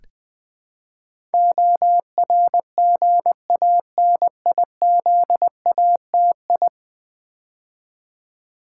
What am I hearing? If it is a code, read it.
Morse code, heard as a keyed tone, about 20 wpm, 700 Hz. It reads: ORGANIZATI
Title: Build A Word - 5 Letters At A Time - 12-Letter Words - 20wpm